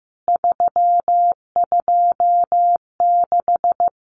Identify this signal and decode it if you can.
Morse code, heard as a keyed tone, about 15 wpm, 700 Hz. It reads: 326
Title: CWT Exchange 15wpm